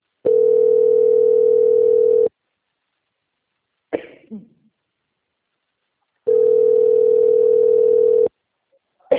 Ringback tone (North America), 440 + 480 Hz.